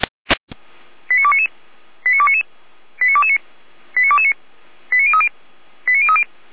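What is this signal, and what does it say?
Five-tone selective call (ZVEI), heard as quick runs of five tones, tones of about 70 ms. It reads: 89299 89299 89229 89229 89030 89030
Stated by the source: Five-tone ZVEI selective call 89240 received on an SDR, from feuerwehr-orlamuende.org fire brigade server.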